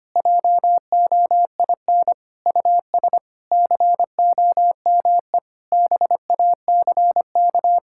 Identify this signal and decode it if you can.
Morse code, heard as a keyed tone, about 25 wpm, 700 Hz. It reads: JOIN US COME BACK
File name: practice / 25wpm / Sentences with 4-Word Phrases - 25wpm